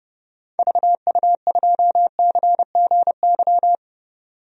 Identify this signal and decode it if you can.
Morse code, heard as a keyed tone, about 30 wpm, 700 Hz. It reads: VU2CGY
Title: DXCC-100 Call Signs - 6 Characters - 30wpm